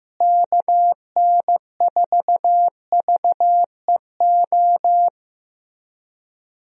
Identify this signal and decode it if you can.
Morse code, heard as a keyed tone, about 16 wpm, 700 Hz. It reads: KN4VEO
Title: USA Call Signs - 6 Characters - Repeated 3x - 15wpm